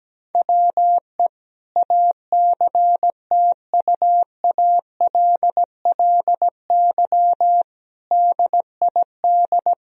Morse code, heard as a keyed tone, about 17 wpm, 700 Hz. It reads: WE ACTUALLY DID
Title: Sentences from Top 1000 Words - Review Part 4 - 17wpm